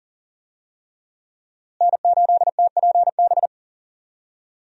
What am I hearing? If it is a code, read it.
Morse code, heard as a keyed tone, about 40 wpm, 700 Hz. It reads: N8TPB
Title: USA Call Signs - 5 Characters - Repeated 3x - 40wpm